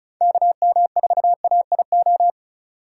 Morse code, heard as a keyed tone, about 34 wpm, 700 Hz. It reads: KM4AIO
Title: USA Call Signs - 4x Character Spacing - 35wpm